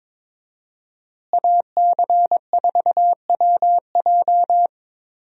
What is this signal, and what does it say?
Morse code, heard as a keyed tone, about 22 wpm, 700 Hz. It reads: AC4WJ
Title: USA Call Signs - 5 Characters - Repeated 2x - 22wpm